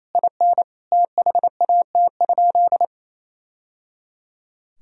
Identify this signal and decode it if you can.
Morse code, heard as a keyed tone, about 28 wpm, 700 Hz. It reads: IN THAT?
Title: Sentences from Top 1000 Words - Review Part 4 - 28wpm